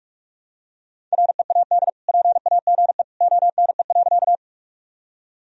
Morse code, heard as a keyed tone, about 47 wpm, 700 Hz. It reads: READ PAGE ONE.
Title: Sentences from Top 1000 Words - Review Part 1 - 45wpm